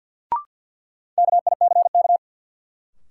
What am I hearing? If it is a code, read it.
Morse code, heard as a keyed tone, about 53 wpm, 700 Hz. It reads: KICK
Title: Sentences from Top 1300 Words - New Word Drill - 50wpm